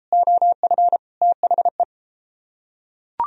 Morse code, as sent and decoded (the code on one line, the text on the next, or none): --- ..-. / - .... .
OF THE